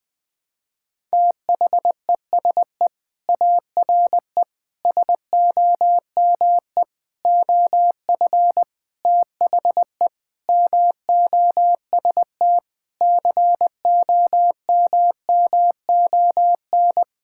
Morse code, heard as a keyed tone, about 20 wpm, 700 Hz. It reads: THESE ARE SOME OF THE MOST COMMON
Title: Sentences with 4-Word Phrases - Spoken Beginning - 20wpm